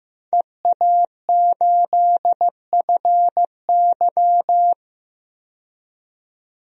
Morse code, heard as a keyed tone, about 15 wpm, 700 Hz. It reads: EA8FY
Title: DXCC-100 Call Signs - 5 Characters - 15wpm